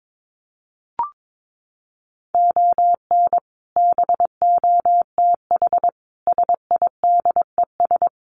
Morse code, heard as a keyed tone, about 22 wpm, 700 Hz. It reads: ON BOTH SIDES
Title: Sets of 4 Words 22wpm